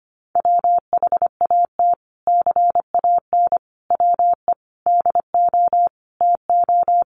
Morse code, heard as a keyed tone, about 25 wpm, 700 Hz. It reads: WHAT CAN WE DO TO
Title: Sentences from Top 1100 Words 25wpm